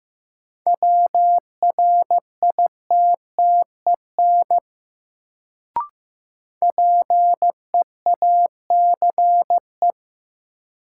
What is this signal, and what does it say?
Morse code, heard as a keyed tone, about 15 wpm, 700 Hz. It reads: WRITTEN PEACE
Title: Sentences from Top 1500 Words - New Word Drill Alt - 15wpm